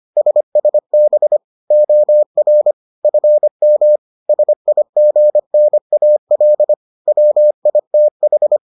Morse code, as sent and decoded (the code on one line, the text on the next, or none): ... ... -... / --- .-. / ..-. -- / ... .. --. -. .- .-.. / .-- .. - ....
SSB OR FM SIGNAL WITH